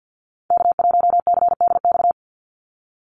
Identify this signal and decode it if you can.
Morse code, heard as a keyed tone, about 49 wpm, 700 Hz. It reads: K1CDX
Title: Call Signs - 50wpm